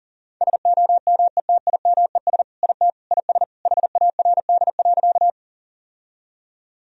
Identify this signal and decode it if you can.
Morse code, heard as a keyed tone, about 40 wpm, 700 Hz. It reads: SOMETIMES IT IS HARD.
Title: Sentences from Top 1000 Words - Review Part 3 - 40wpm